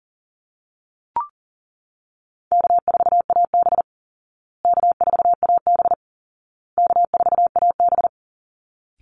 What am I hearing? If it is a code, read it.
Morse code, heard as a keyed tone, about 40 wpm, 700 Hz. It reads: K4AB K4AB K4AB
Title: Easy Call Signs Repeated 3x - 40wpm